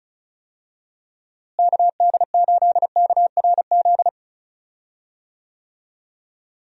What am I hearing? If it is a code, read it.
Morse code, heard as a keyed tone, about 36 wpm, 700 Hz. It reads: KD8KRZ
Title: USA Call Signs - 6 Characters - Repeated 2x - 35wpm